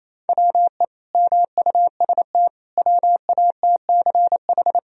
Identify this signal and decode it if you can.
Morse code, heard as a keyed tone, about 28 wpm, 700 Hz. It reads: WE MUST WATCH
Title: Sentences from Top 1000 Words - Review Part 1 - 28wpm